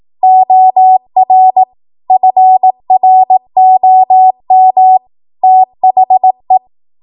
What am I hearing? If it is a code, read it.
Morse code, heard as a keyed tone, about 18 wpm, 750 Hz. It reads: OR FROM THE
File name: Archive / 18 WPM / 18WPM